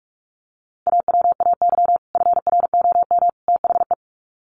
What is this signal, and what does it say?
Morse code, heard as a keyed tone, about 44 wpm, 700 Hz. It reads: AWAY FROM THE
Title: Sets of 3 Words 45wpm